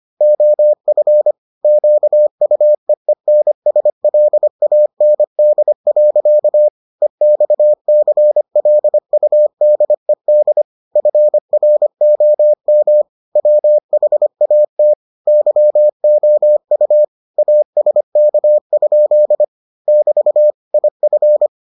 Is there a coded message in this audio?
Yes: Morse code from 0.2 to 21.5 s, heard as a keyed tone, about 25 wpm, 600 Hz. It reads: OF QUEENSLAND. EXCLUDED FROM WHAT YOU ASK? <BT> IF